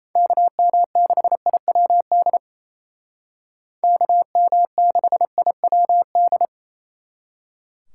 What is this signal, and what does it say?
Morse code, heard as a keyed tone, about 33 wpm, 700 Hz. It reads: KM6IWD KM6IWD
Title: Speed Racing - USA 2x3 Call Signs 28wpm